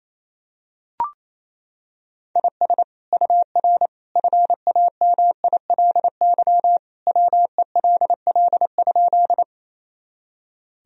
Morse code, heard as a keyed tone, about 28 wpm, 700 Hz. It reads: IS UR FAMILY WELL?
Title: Sentences from Top 400 Words and Abbreviations 28wpm